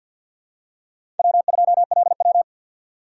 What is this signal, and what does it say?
Morse code, heard as a keyed tone, about 52 wpm, 700 Hz. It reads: W2LW
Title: USA Call Signs - 3x Character Spacing - 50wpm